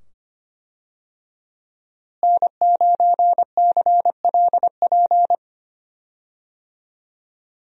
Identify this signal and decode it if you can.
Morse code, heard as a keyed tone, about 25 wpm, 700 Hz. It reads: N9CLP
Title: USA Call Signs - 5 Characters - Repeated 2x - 25wpm